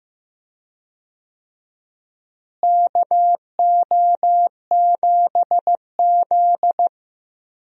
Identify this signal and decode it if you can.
Morse code, heard as a keyed tone, about 15 wpm, 700 Hz. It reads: KO7Z
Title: USA Call Signs - 4 Characters - Repeated 2x - 15wpm